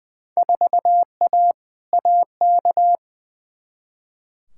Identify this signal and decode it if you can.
Morse code, heard as a keyed tone, about 21 wpm, 700 Hz. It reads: 4A AK
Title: Field Day Exchanges Encore - 20wpm